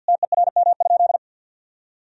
Morse code, heard as a keyed tone, about 50 wpm, 700 Hz. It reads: TERM?
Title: Sentences from Top 1000 Words - Review Part 4 - 50wpm